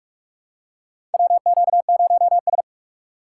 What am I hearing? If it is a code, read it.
Morse code, heard as a keyed tone, about 46 wpm, 700 Hz. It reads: WQ0S
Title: USA Call Signs - 4 Characters - 8x Character Spacing - 45wpm